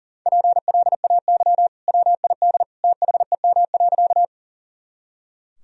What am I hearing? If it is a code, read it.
Morse code, heard as a keyed tone, about 40 wpm, 700 Hz. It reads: PLAY WID THEM.
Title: Sentences from Top 200 Words and Abbreviations 40wpm